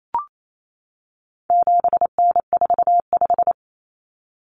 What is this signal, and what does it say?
Morse code, heard as a keyed tone, about 28 wpm, 700 Hz. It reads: 7N45
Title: Common Cut Numbers - 4-Digit 28wpm